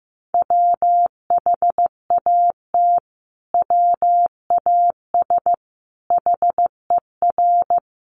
Morse code, heard as a keyed tone, about 15 wpm, 700 Hz. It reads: WHAT WAS HER
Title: Sentences from Top 1400 Words 15wpm